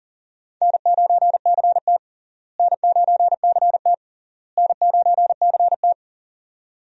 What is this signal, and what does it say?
Morse code, heard as a keyed tone, about 40 wpm, 700 Hz. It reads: N9CT N9CT N9CT
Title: USA Call Signs - 4 Characters - Repeated 3x - 40wpm